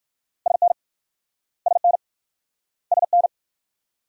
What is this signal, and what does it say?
Morse code, heard as a keyed tone, about 54 wpm, 690 Hz. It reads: SN SN SN
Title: Speed-Racing Multi-letter Phonemes - 45wpm